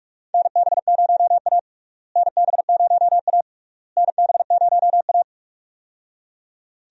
Morse code, heard as a keyed tone, about 46 wpm, 700 Hz. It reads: NB0A NB0A NB0A
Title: USA Call Signs - 4 Characters - Repeated 3x - 45wpm